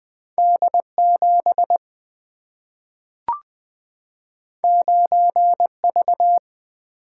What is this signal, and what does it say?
Morse code, heard as a keyed tone, about 20 wpm, 700 Hz. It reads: D 7 9 V
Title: Single-Letter Single-Number 20wpm